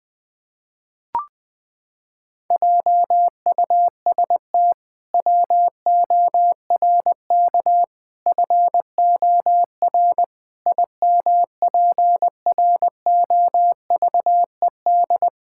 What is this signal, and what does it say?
Morse code, heard as a keyed tone, about 21 wpm, 700 Hz. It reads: JUST WORK FOR IMPROVED
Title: Pangram Challenge v2 - 20wpm